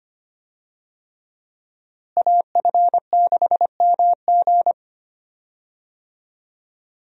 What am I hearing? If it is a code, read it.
Morse code, heard as a keyed tone, about 25 wpm, 700 Hz. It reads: AF6MG
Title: USA Call Signs - 5 Characters - Repeated 3x - 25wpm